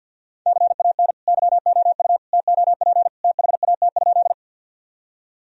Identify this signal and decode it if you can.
Morse code, heard as a keyed tone, about 52 wpm, 700 Hz. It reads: CAN YOU TOP THAT?
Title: Sentences from Top 1000 Words - Review Part 2 - 50wpm